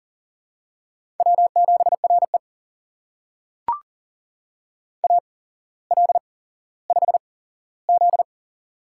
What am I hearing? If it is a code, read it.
Morse code, heard as a keyed tone, about 41 wpm, spaced as for 17 wpm, 700 Hz. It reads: W7RE AL5Z
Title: USA Call Signs - 4 Characters - 8x Character Spacing - 40wpm